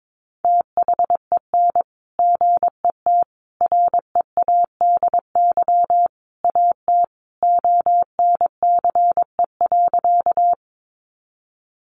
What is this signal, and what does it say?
Morse code, heard as a keyed tone, about 22 wpm, 700 Hz. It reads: THEN GET READY AT ONCE.
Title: Sentences from Top 400 Words 22wpm